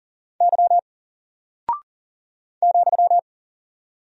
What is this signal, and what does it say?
Morse code, heard as a keyed tone, about 40 wpm, 700 Hz. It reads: Y ,
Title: Single Character (Letter:Number:Punctuation:Prosign) Alt 40wpm